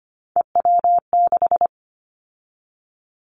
Morse code, heard as a keyed tone, about 25 wpm, 700 Hz. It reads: EW6
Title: Callsign Prefixes for Top 100 DXCC - 25wpm